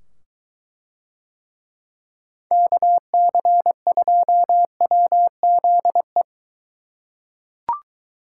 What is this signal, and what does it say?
Morse code, heard as a keyed tone, about 21 wpm, 700 Hz. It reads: KC2WZE